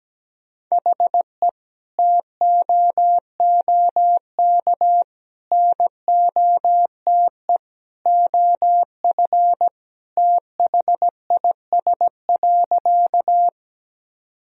Morse code, heard as a keyed tone, about 17 wpm, 700 Hz. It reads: HE TOOK NOTE OF THIS.
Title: Sentences from Top 1000 Words - Review Part 2 - 17wpm